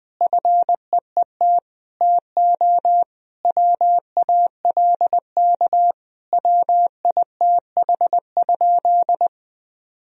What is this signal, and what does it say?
Morse code, heard as a keyed tone, about 21 wpm, 700 Hz. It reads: FEET TO WALK WITH?
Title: Sentences from Top 1000 Words - Review Part 1 - 20wpm